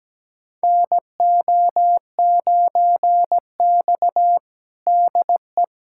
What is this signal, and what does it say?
Morse code, heard as a keyed tone, about 17 wpm, 700 Hz. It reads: NO9X DE
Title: USA Callsign DE Callsign - 4 Characters - 17wpm